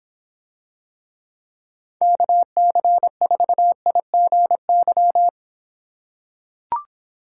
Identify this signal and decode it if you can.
Morse code, heard as a keyed tone, about 26 wpm, 700 Hz. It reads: KC4IGY